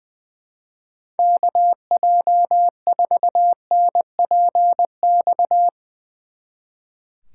Morse code, heard as a keyed tone, about 20 wpm, 700 Hz. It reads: KJ4NPX
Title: Speed Racing - USA 2x3 Call Signs 20wpm